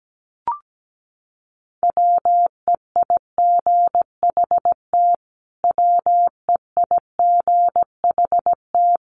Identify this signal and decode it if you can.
Morse code, heard as a keyed tone, about 17 wpm, 700 Hz. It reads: WEIGHT WEIGHT
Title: Top 500 Words Repeated 17wpm